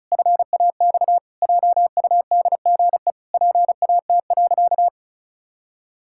Morse code, heard as a keyed tone, about 36 wpm, 700 Hz. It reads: FAX JUDGE PAT.
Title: Pangram Challenge v2 - 35wpm